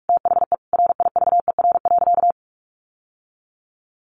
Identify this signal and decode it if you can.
Morse code, heard as a keyed tone, about 44 wpm, 700 Hz. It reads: THE RIVER.
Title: Sentences from Top 1500 Words 45wpm